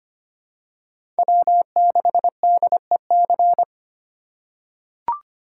Morse code, heard as a keyed tone, about 25 wpm, 700 Hz. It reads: W6DEC
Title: USA Call Signs - 5 Characters - 3x Character Spacing - 25wpm